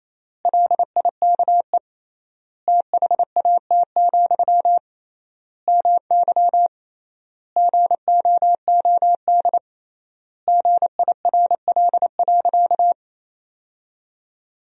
Morse code, heard as a keyed tone, about 28 wpm, 700 Hz. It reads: LIKE THAT, MY GOOD GIRL.
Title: 3x Word-Spacing - Sentences from Top 400 Words - 28wpm